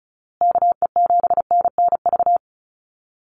Morse code, heard as a keyed tone, about 34 wpm, 700 Hz. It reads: KE7NNV
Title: USA Call Signs Repeated 2x 35wpm